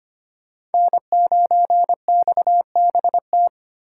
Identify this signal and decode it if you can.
Morse code, heard as a keyed tone, about 33 wpm, 700 Hz. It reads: N9XBT